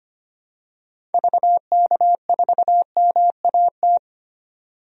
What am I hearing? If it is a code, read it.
Morse code, heard as a keyed tone, about 25 wpm, 690 Hz. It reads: VK4MAT
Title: DXCC-100 Call Signs - 6 Characters - 25wpm